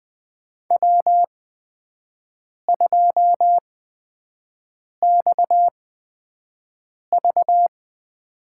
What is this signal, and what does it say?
Morse code, heard as a keyed tone, about 20 wpm, spaced as for 5 wpm, 700 Hz. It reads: W2XV
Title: USA Call Signs - 4 Characters - 8x Character Spacing - 20wpm